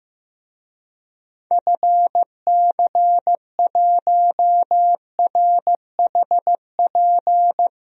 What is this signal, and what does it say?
Morse code, heard as a keyed tone, about 15 wpm, 700 Hz. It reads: FC1RHP
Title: DXCC-100 Call Signs - 6 Characters - Repeated 3x - 15wpm